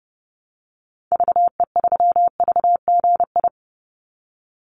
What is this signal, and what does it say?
Morse code, heard as a keyed tone, about 30 wpm, 700 Hz. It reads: VE3VGI
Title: Call Signs Repeated 3x - 30wpm